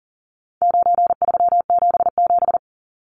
Morse code, heard as a keyed tone, about 40 wpm, 700 Hz. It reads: 9377
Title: Numbers - 4-Digit 40wpm